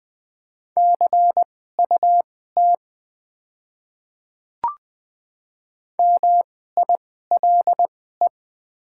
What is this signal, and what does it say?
Morse code, heard as a keyed tone, about 20 wpm, spaced as for 14 wpm, 700 Hz. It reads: CUT MILE
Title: 2x Character Spacing - Top 300 Words - 20wpm